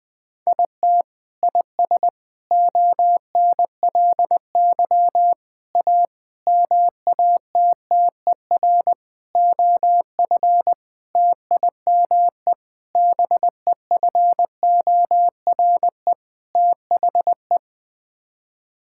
Morse code, heard as a keyed tone, about 21 wpm, 700 Hz. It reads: IT IS ONLY A MATTER OF TIME BEFORE THE
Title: Sets of 9 Words - 20wpm